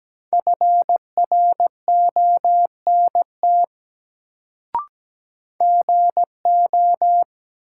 Morse code, heard as a keyed tone, about 17 wpm, 700 Hz. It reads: FRONT GO
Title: Sets of 2 Words - New Word Drill - 17wpm